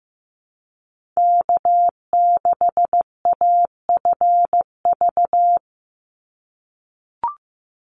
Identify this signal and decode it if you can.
Morse code, heard as a keyed tone, about 15 wpm, 700 Hz. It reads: K6AFV
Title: USA Call Signs 15wpm